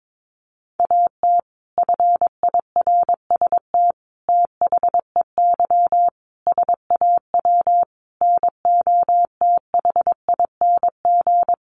Morse code, heard as a keyed tone, about 22 wpm, 700 Hz. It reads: AT FIRST THEY SAW NOTHING
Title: Sentences from Top 600 Words 22wpm